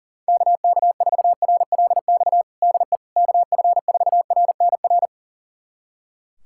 Morse code, heard as a keyed tone, about 40 wpm, 700 Hz. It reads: KK4RLX DE KF4RNR
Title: USA Callsign DE Callsign - 6 Characters - 40wpm